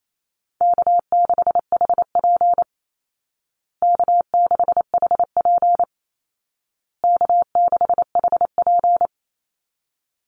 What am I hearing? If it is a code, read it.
Morse code, heard as a keyed tone, about 28 wpm, 700 Hz. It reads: K6HP K6HP K6HP
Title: Easy Call Signs Repeated 3x - 28wpm